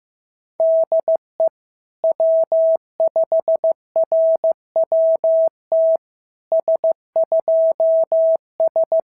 Morse code, heard as a keyed tone, about 16 wpm, 650 Hz. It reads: DE W5RWT S2S
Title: SOTA QSO Practice 15wpm